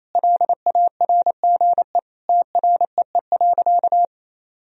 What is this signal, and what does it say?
Morse code, heard as a keyed tone, about 27 wpm, 700 Hz. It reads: LARGE TREE.
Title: Sentences from Top 1000 Words - Review Part 4 - 28wpm